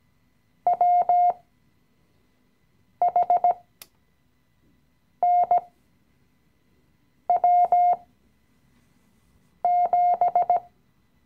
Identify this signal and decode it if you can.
Morse code, heard as a keyed tone, about 17 wpm, spaced as for 5 wpm, 710 Hz. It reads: WHNW7
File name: CW-Online-Kurs / Anfänger / Lektion 017